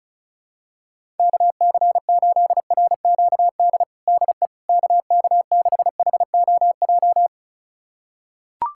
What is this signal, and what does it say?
Morse code, heard as a keyed tone, about 36 wpm, 700 Hz. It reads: KC8RQD DE KK6HOJ
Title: USA Callsign DE Callsign - 6 Characters - 35wpm